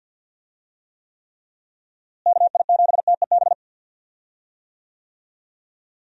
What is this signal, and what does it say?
Morse code, heard as a keyed tone, about 46 wpm, 700 Hz. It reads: KI7TEB
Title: USA Call Signs - 6 Characters - Repeated 3x - 50wpm